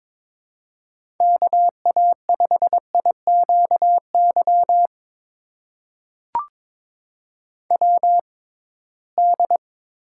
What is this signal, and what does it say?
Morse code, heard as a keyed tone, about 23 wpm, spaced as for 17 wpm, 700 Hz. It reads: KA5IQY WD
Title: USA Call Signs - 6 Characters - 6x Character Spacing - 22wpm